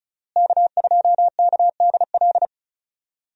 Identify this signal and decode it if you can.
Morse code, heard as a keyed tone, about 34 wpm, 700 Hz. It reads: K2KDL
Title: USA Call Signs - 5 Characters - Repeated 2x - 35wpm